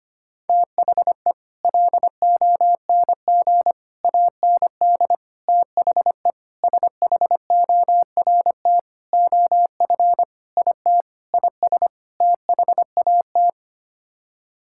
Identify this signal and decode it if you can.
Morse code, heard as a keyed tone, about 25 wpm, 700 Hz. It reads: THE LONG AND THE SHORT OF IT IS THAT
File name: Sets of 9 Words - 25wpm